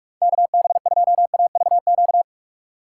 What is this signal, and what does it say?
Morse code, heard as a keyed tone, about 44 wpm, 700 Hz. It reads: KB2AVQ
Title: USA Call Signs - 6 Characters - 2x Character Spacing - 45wpm